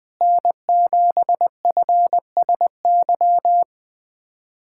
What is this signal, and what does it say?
Morse code, heard as a keyed tone, about 20 wpm, 700 Hz. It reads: N7FSY
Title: USA Callsign DE Callsign - 5 Characters - 20wpm